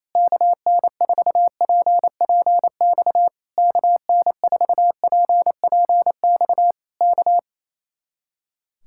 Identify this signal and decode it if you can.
Morse code, heard as a keyed tone, about 27 wpm, 700 Hz. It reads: KN4PPX KN4PPX K
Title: Speed-Racing Common QSO Phrases - 28wpm